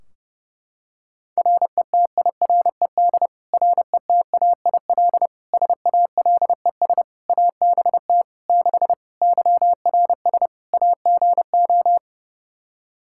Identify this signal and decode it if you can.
Morse code, heard as a keyed tone, about 30 wpm, 700 Hz. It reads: RETIRED RETAIL SALES ABT 6 YRS AGO
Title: Practice QSO - Exchange 3 - More Info - 30wpm